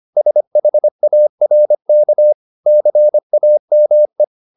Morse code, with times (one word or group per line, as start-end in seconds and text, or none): 0.2-2.3 SHARK
2.7-4.2 CAME